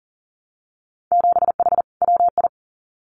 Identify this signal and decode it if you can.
Morse code, heard as a keyed tone, about 40 wpm, 700 Hz. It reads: 7H WI
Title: Winter Field Day Exchanges Encore 40wpm